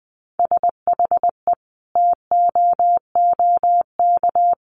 Morse code, heard as a keyed tone, about 20 wpm, 700 Hz. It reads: SHE TOOK
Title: Sentences from Top 1500 Words 20wpm